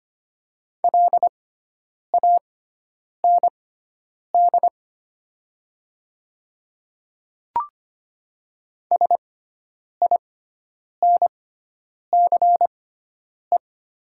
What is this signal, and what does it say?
Morse code, heard as a keyed tone, about 25 wpm, spaced as for 9 wpm, 700 Hz. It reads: LAND SINCE
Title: 6x Character Spacing - Top 300 Words - 25wpm